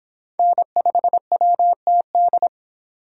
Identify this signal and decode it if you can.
Morse code, heard as a keyed tone, about 23 wpm, 700 Hz. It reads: N5WTD